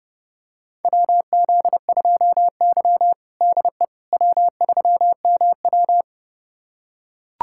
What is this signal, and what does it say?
Morse code, heard as a keyed tone, about 32 wpm, 700 Hz. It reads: WZ2Y DE W3MW
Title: USA Callsign DE Callsign - 4 Characters - 30wpm